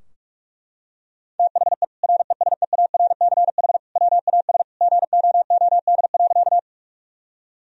Morse code, heard as a keyed tone, about 42 wpm, 700 Hz. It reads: THE RESEARCH WAS GOOD.
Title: Sentences from Top 1000 Words - Review Part 4 - 45wpm